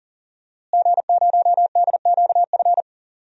Morse code, heard as a keyed tone, about 40 wpm, 700 Hz. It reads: G0DQF
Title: DXCC-100 Call Signs - 5 Characters - Repeated 3x - 40wpm